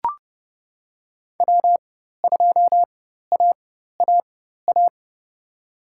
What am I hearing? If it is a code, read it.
Morse code, heard as a keyed tone, about 30 wpm, spaced as for 14 wpm, 700 Hz. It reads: W2AAA
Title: USA Call Signs - 5 Characters - 4x Character Spacing - 30wpm